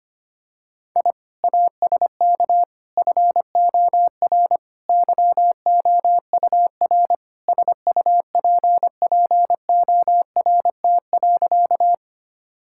Morse code, heard as a keyed tone, about 25 wpm, 700 Hz. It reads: I ASK FOR YOUR SUPPORT.
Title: Sentences from Top 1000 Words - Review Part 5 - 25wpm